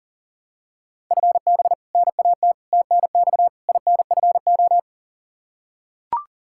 Morse code, heard as a keyed tone, about 40 wpm, 700 Hz. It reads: FB NAT TNX INFO
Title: Practice QSO - Exchange 3 - More Info - 40wpm